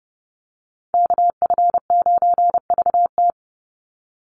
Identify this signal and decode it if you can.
Morse code, heard as a keyed tone, about 30 wpm, 700 Hz. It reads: KF9VT
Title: USA Call Signs 30wpm